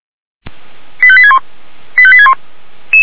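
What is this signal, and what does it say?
Five-tone selective call (ZVEI), heard as quick runs of five tones, tones of about 70 ms. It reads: 85861 85861
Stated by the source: ZVEI five-tone selective call received on an SDR.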